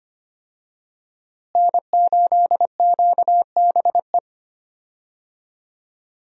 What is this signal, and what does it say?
Morse code, heard as a keyed tone, about 25 wpm, 700 Hz. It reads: N8QBE
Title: USA Call Signs - 5 Characters - Repeated 2x - 25wpm